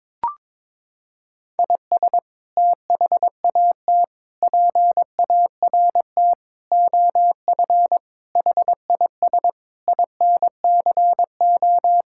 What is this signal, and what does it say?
Morse code, heard as a keyed tone, about 22 wpm, 700 Hz. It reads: IS THAT PART OF HIS INCO
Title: Sentences from Top 1000 Words - Review Part 5 - 22wpm